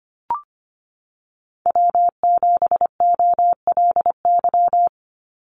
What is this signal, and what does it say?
Morse code, heard as a keyed tone, about 25 wpm, 700 Hz. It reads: W7OLY
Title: Call Signs Repeated 2x - 25wpm